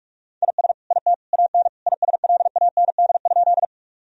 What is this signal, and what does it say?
Morse code, heard as a keyed tone, about 42 wpm, 700 Hz. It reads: IS IT AN ISLAND?
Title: Sentences from Top 1000 Words - Review Part 3 - 45wpm